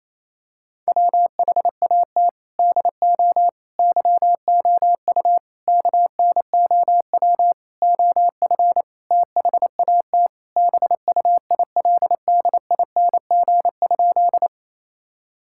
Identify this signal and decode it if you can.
Morse code, heard as a keyed tone, about 28 wpm, 700 Hz. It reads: WHAT DO YOU KNOW OF THAT BUILDING?
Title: Sentences from Top 1000 Words - Review Part 5 - 28wpm